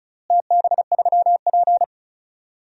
Morse code, heard as a keyed tone, about 36 wpm, 700 Hz. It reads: TB3P
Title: DXCC-100 Call Signs - 4 Characters - Repeated 3x - 35wpm